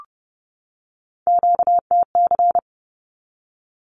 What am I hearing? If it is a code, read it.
Morse code, heard as a keyed tone, about 30 wpm, 700 Hz. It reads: QTC
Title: Q-Codes 30wpm